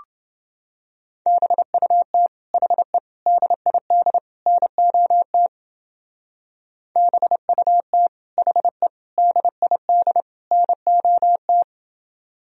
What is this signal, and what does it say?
Morse code, heard as a keyed tone, about 30 wpm, 700 Hz. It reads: BUT HE DID NOT BUT HE DID NOT